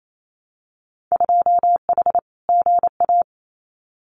Morse code, heard as a keyed tone, about 28 wpm, 700 Hz. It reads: 2H GA
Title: Winter Field Day Exchanges 28wpm